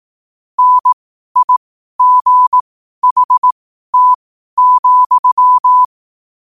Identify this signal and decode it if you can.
Morse code, heard as a keyed tone, about 18 wpm, spaced as for 12 wpm, 1000 Hz. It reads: NIGHT,
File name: qotdfiles / 12wpm